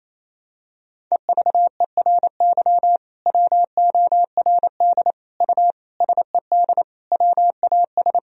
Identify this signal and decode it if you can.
Morse code, heard as a keyed tone, about 28 wpm, 700 Hz. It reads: EVERY WORD U SED WAS
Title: Sentences from Top 400 Words and Abbreviations 28wpm